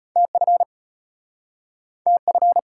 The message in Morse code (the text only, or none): TF TF